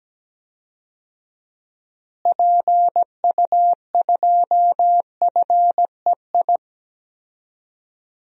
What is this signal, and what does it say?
Morse code, heard as a keyed tone, about 17 wpm, 700 Hz. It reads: PU2FEI
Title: DXCC-100 Call Signs - 6 Characters - Repeated 3x - 17wpm